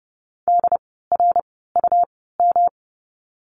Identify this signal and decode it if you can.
Morse code, heard as a keyed tone, about 30 wpm, spaced as for 17 wpm, 700 Hz. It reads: DRUM
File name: course / 30:10wpm / 108 Lesson 18f - Letter m - Words 30wpm